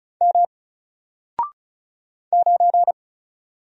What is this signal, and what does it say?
Morse code, heard as a keyed tone, about 36 wpm, 700 Hz. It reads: M 9
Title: Single Character (Letter:Number:Punctuation:Prosign) Alt 35wpm